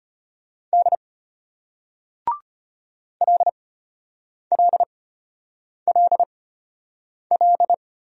Morse code, heard as a keyed tone, about 33 wpm, spaced as for 8 wpm, 700 Hz. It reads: D LLLL